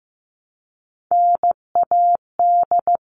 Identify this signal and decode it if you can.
Morse code, heard as a keyed tone, about 15 wpm, 700 Hz. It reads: NAD
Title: All Cut Numbers - 3-Digit 15wpm